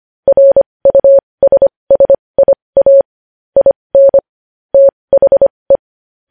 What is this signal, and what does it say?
Morse code, heard as a keyed tone, about 25 wpm, spaced as for 20 wpm, 550 Hz. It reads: RUSSIA IN THE
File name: podcast / News headlines-1x-25x20